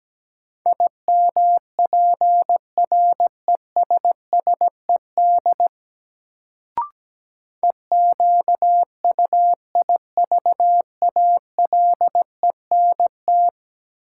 Morse code, heard as a keyed tone, about 17 wpm, 700 Hz. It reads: IMPRESSED EQUIVALENT
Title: Sentences from Top 1600 Words - New Word Drill - 17wpm